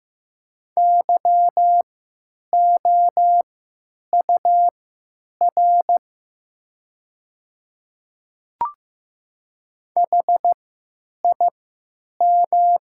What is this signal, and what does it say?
Morse code, heard as a keyed tone, about 15 wpm, spaced as for 9 wpm, 700 Hz. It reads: YOUR HIM
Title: 3x Character Spacing - Top 100 Words - 15wpm